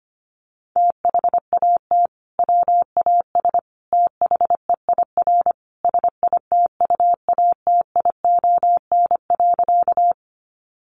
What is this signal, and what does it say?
Morse code, heard as a keyed tone, about 25 wpm, 700 Hz. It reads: THAT WAS THEIR SITUATION.
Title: Sentences from Top 1200 Words 25wpm